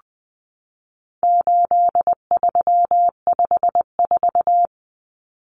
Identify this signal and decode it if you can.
Morse code, heard as a keyed tone, about 20 wpm, 700 Hz. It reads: 8354
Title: Numbers - 4-Digit 20wpm